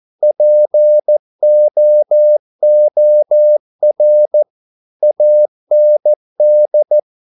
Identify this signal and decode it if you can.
Morse code, heard as a keyed tone, about 14 wpm, 600 Hz. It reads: POOR AND
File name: mp3 files / a dogs mistake - 14wpm